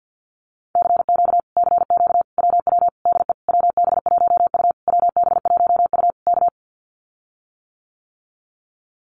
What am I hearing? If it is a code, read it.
Morse code, heard as a keyed tone, about 50 wpm, 700 Hz. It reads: CQ CQ WW DE WB1U WB1U K
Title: Common QSO Phrases Encore 50wpm